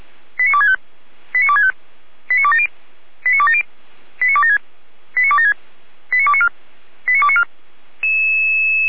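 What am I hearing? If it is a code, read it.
Five-tone selective call (ZVEI), heard as quick runs of five tones, tones of about 70 ms. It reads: 89265 89265 89270 89270 89276 89276 89294 89294
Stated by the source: ZVEI five-tone call received by SDR, Feuerwehr Orlamünde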